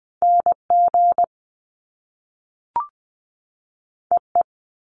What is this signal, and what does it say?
Morse code, heard as a keyed tone, about 20 wpm, 700 Hz. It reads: N G E E
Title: Top Two Letter Combinations in Words 20wpm